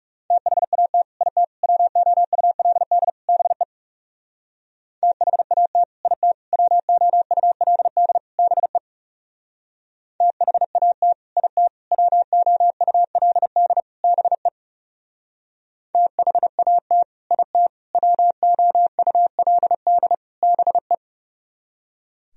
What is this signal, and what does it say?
Morse code, heard as a keyed tone, about 38 wpm, 700 Hz. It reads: THAT IT WOULD BE THAT IT WOULD BE THAT IT WOULD BE THAT IT WOULD BE
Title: Speed-Racing - Sets of 4 Words 30wpm